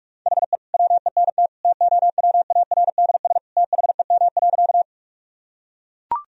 Morse code, heard as a keyed tone, about 41 wpm, 700 Hz. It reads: HE WENT TOWARDS THEM.
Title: Sentences from Top 1000 Words - Review Part 2 - 45wpm